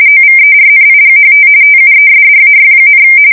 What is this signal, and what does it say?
RTTY, about 45.45 baud. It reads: OVERNIGHT TUESDAY,
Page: 170 Shift 2125/2295